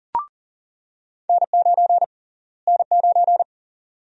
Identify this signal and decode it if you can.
Morse code, heard as a keyed tone, about 40 wpm, 700 Hz. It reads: N9 N9
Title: USA Call Sign Prefixes - Repeated 2x - 40wpm